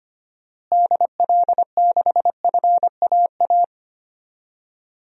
Morse code, heard as a keyed tone, about 25 wpm, 700 Hz. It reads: DL6FAA
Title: DXCC-100 Call Signs - 6 Characters - Repeated 3x - 25wpm